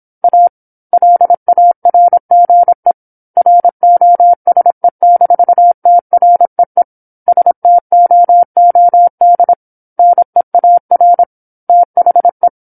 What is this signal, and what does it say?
Morse code, heard as a keyed tone, about 27 wpm, 700 Hz. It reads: A LARGE ROSE-TREE STOOD NEAR THE
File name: 08-Chapter-08 26 WPM